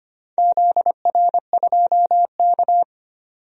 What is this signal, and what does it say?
Morse code, heard as a keyed tone, about 25 wpm, 700 Hz. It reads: ZR2K
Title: DXCC-100 Call Signs - 4 Characters - 25wpm